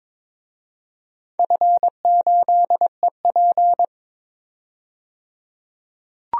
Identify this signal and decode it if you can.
Morse code, heard as a keyed tone, about 22 wpm, 700 Hz. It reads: F8EP
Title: DXCC-100 Call Signs - 4 Characters - 22wpm